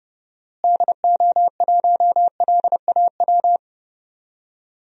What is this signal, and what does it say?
Morse code, heard as a keyed tone, about 30 wpm, 700 Hz. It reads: DO1LAW
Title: DXCC-100 Call Signs - 6 Characters - Repeated 2x - 30wpm